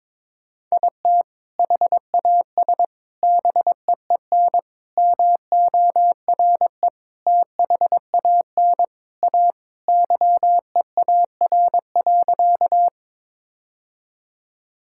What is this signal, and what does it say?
Morse code, heard as a keyed tone, about 22 wpm, 700 Hz. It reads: IT HAS BEEN MORE THAN A YEAR.
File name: practice / 22wpm / Sentences with 4-Word Phrases - 22wpm